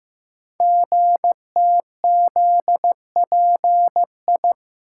Morse code, heard as a keyed tone, about 15 wpm, 700 Hz. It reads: GTZPI
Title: Code Groups - 5 Letters - 15wpm